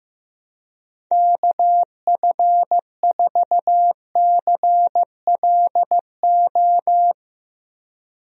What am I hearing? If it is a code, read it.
Morse code, heard as a keyed tone, about 15 wpm, 700 Hz. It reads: KF4CLO
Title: USA Call Signs - 6 Characters - Repeated 3x - 15wpm